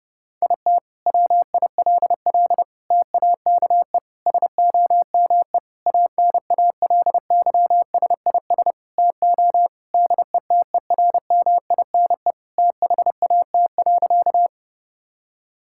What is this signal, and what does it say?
Morse code, heard as a keyed tone, about 32 wpm, 700 Hz. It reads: IT WILL TAKE SOME ANALYSIS TO DETERMINE THAT.
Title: Sentences from Top 1000 Words - Review Part 5 - 30wpm